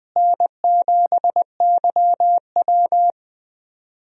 Morse code, heard as a keyed tone, about 20 wpm, 700 Hz. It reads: N7YW
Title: USA Call Signs - 4 Characters - Repeated 3x - 20wpm